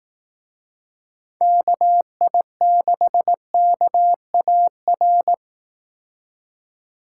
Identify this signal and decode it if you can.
Morse code, heard as a keyed tone, about 18 wpm, 700 Hz. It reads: KI6KAR
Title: Speed Racing - USA 2x3 Call Signs 15wpm